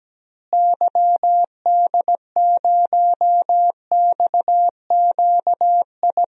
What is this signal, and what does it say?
Morse code, heard as a keyed tone, about 17 wpm, 700 Hz. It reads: YD0XQI
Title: DXCC-100 Call Signs - 6 Characters - Repeated 2x - 17wpm